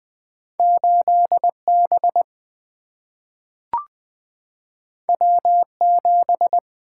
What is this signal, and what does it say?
Morse code, heard as a keyed tone, about 20 wpm, 700 Hz. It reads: 8 B W 7
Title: Single-Letter Single-Number 20wpm